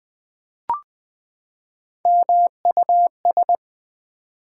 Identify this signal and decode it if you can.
Morse code, heard as a keyed tone, about 20 wpm, 700 Hz. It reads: MUS
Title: Build A Word - 3 Letters At A Time - 6-Letter Words - 20wpm